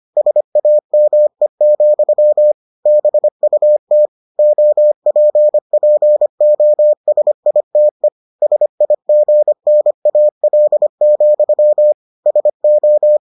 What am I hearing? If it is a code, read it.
Morse code, heard as a keyed tone, about 25 wpm, 600 Hz. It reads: SAME, BUT OPPOSITE SIGNAL, SO